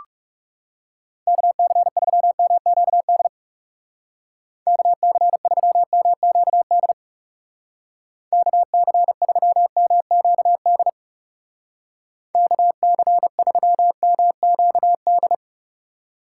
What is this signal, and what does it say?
Morse code, heard as a keyed tone, about 36 wpm, 700 Hz. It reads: KC3MQD KC3MQD KC3MQD KC3MQD
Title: Speed Racing - USA Call Signs 30wpm